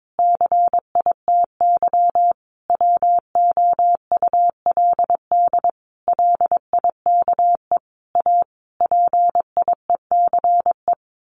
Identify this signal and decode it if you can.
Morse code, heard as a keyed tone, about 22 wpm, 700 Hz. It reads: CITY WOULD LIKE A PIECE
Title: Sentences from Top 1100 Words 22wpm